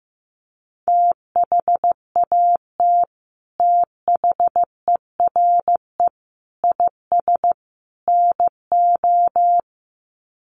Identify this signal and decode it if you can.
Morse code, heard as a keyed tone, about 14 wpm, 700 Hz. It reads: THAT THERE IS NO
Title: Sets of 4 Words 15wpm